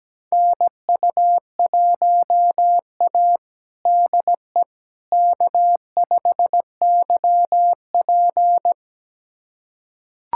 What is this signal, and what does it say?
Morse code, heard as a keyed tone, about 17 wpm, 700 Hz. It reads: NU1A DE K5YP
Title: USA Callsign DE Callsign - 4 Characters - 17wpm